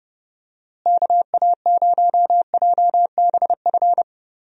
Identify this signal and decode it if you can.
Morse code, heard as a keyed tone, about 30 wpm, 700 Hz. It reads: KA0JBF
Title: USA Call Signs - 6 Characters - Repeated 3x - 30wpm